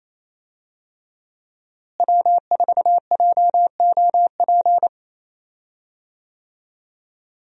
Morse code, heard as a keyed tone, about 27 wpm, 700 Hz. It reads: W4JOP
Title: USA Call Signs - 5 Characters - Repeated 3x - 28wpm